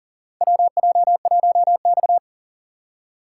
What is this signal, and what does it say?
Morse code, heard as a keyed tone, about 40 wpm, 700 Hz. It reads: WJ1X
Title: USA Call Signs - 4 Characters - 3x Character Spacing - 40wpm